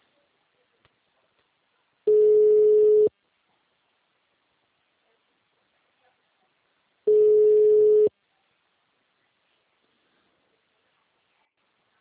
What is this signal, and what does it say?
Ringback tone (Europe, 425 Hz).